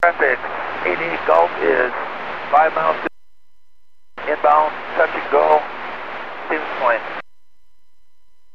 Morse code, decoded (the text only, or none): none